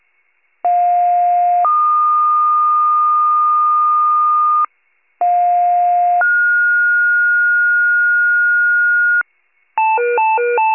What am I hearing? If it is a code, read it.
Two-tone page, heard as a one-second tone, then a three-second tone, about 710 then 1220 Hz, about 710 then 1430 Hz, with noise. It is heard from 0.6 to 9.2 s.